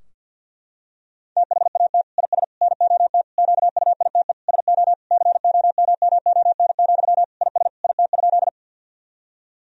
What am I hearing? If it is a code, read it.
Morse code, heard as a keyed tone, about 50 wpm, 700 Hz. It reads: THAT IS NOT QUITE SO COMMON, IS IT?
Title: Sentences from Top 1000 Words - Review Part 3 - 50wpm